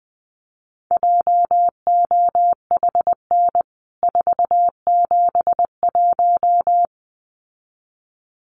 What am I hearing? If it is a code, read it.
Morse code, heard as a keyed tone, about 20 wpm, 700 Hz. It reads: JOHN 471
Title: CWT Exchange 20wpm